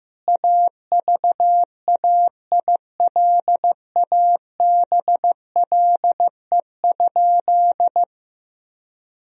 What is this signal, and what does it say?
Morse code, heard as a keyed tone, about 15 wpm, 700 Hz. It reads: AVAILABLE?
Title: Sentences from Top 1000 Words - Review Part 3 - 15wpm